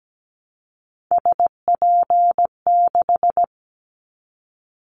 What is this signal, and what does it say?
Morse code, heard as a keyed tone, about 17 wpm, 700 Hz. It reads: SP6
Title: Callsign Prefixes for Top 25 DXCC - Repeated 3x - 17wpm